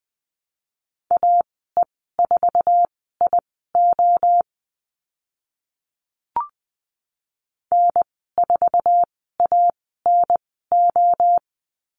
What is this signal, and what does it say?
Morse code, heard as a keyed tone, about 20 wpm, spaced as for 14 wpm, 700 Hz. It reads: AE4IO N4ANO
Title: 055 Lesson 10g - Letter r - Call Signs 20wpm